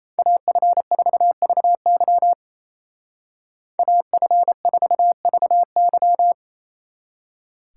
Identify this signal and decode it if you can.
Morse code, heard as a keyed tone, about 33 wpm, 700 Hz. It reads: AF4VY AF4VY
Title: Speed Racing - USA 2x2 Call Signs 28wpm